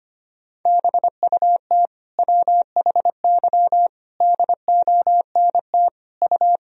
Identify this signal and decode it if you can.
Morse code, heard as a keyed tone, about 25 wpm, 700 Hz. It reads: BUT WHY DONT U
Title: Sentences from Top 300 Words and Abbreviations 25wpm